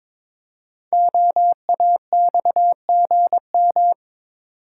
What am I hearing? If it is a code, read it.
Morse code, heard as a keyed tone, about 22 wpm, 700 Hz. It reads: OAXGM
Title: Code Groups - 5 Letters - 22wpm